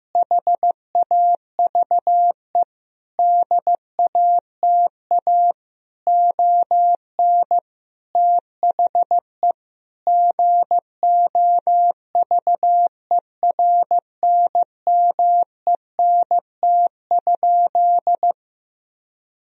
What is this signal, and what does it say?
Morse code, heard as a keyed tone, about 15 wpm, 700 Hz. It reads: HAVE DATA ON THE GOVERNMENT?
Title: Sentences from Top 1000 Words - Review Part 3 - 15wpm